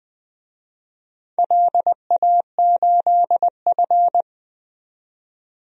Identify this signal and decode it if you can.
Morse code, heard as a keyed tone, about 20 wpm, 700 Hz. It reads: LA8F
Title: DXCC-100 Call Signs - 4 Characters - 20wpm